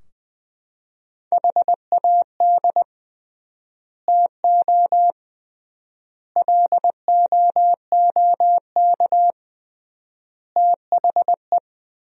Morse code, heard as a keyed tone, about 20 wpm, 700 Hz. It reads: HAD TO LOOK THE
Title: 3x Word-Spacing - Sentences from Top 100 Words - 20wpm